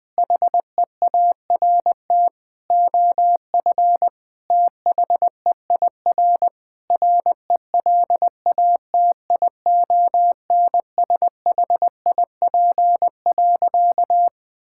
Morse code, heard as a keyed tone, about 21 wpm, 700 Hz. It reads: HEART OF THEIR RELATIONSHIP.
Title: Sentences with 5-Word Phrases - 20wpm